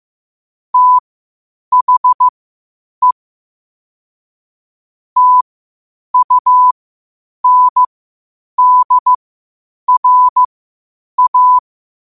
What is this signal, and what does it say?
Morse code, heard as a keyed tone, about 15 wpm, spaced as for 8 wpm, 1000 Hz. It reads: THE TUNDRA
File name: qotdfiles / 8wpm